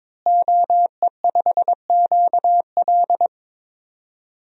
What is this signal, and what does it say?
Morse code, heard as a keyed tone, about 22 wpm, 700 Hz. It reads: OE5QL
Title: DXCC-100 Call Signs - 5 Characters - Repeated 2x - 22wpm